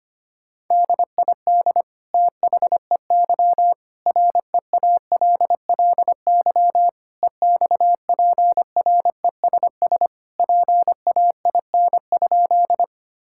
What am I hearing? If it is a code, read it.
Morse code, heard as a keyed tone, about 25 wpm, 700 Hz. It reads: DID THEY REALLY EXPRESS PAIN?
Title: Sentences from Top 1600 Words 25wpm